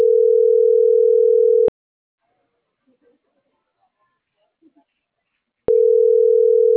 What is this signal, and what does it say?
Ringback tone (North America), 440 + 480 Hz.